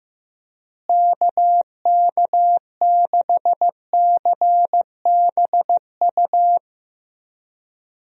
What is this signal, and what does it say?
Morse code, heard as a keyed tone, about 15 wpm, 700 Hz. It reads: KK6CBU
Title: USA Call Signs - 6 Characters - Repeated 3x - 15wpm